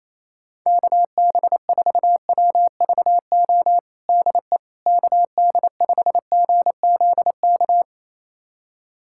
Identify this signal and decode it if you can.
Morse code, heard as a keyed tone, about 28 wpm, 700 Hz. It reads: KB4WVO DE KD5GZK
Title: USA Callsign DE Callsign 28wpm